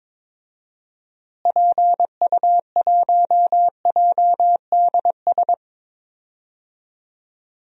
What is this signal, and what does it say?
Morse code, heard as a keyed tone, about 22 wpm, 700 Hz. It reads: PU1JDS
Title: DXCC-100 Call Signs - 6 Characters - Repeated 2x - 22wpm